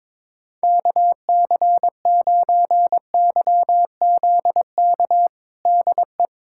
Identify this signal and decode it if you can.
Morse code, heard as a keyed tone, about 22 wpm, 700 Hz. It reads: KC9YZK DE
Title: USA Callsign DE Callsign 22wpm